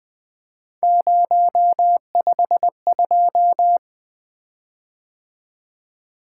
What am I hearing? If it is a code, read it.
Morse code, heard as a keyed tone, about 20 wpm, 700 Hz. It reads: 052
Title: Three Letter-Number Combinations - 20wpm